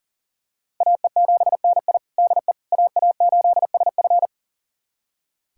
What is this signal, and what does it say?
Morse code, heard as a keyed tone, about 38 wpm, 700 Hz. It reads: AE7NI DE AA8SF
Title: USA Callsign DE Callsign - 5 Characters - 40wpm